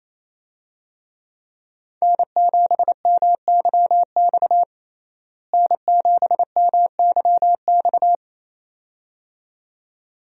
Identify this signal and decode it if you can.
Morse code, heard as a keyed tone, about 28 wpm, 700 Hz. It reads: N7MYX N7MYX
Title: USA Call Signs - 5 Characters - Repeated 2x - 28wpm